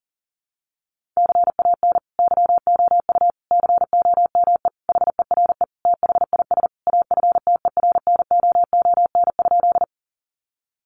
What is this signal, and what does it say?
Morse code, heard as a keyed tone, about 40 wpm, 700 Hz. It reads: CAN YOU COME HERE THIS AFTERNOON?
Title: Sentences from Top 1500 Words 40wpm